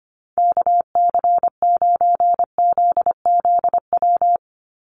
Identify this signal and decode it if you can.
Morse code, heard as a keyed tone, about 25 wpm, 700 Hz. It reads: KC9ZZW
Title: USA Call Signs 25wpm